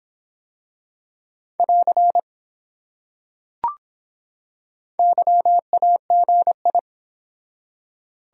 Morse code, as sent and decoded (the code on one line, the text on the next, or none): .-.-. / -.-- .- --. ..
<AR> YAGI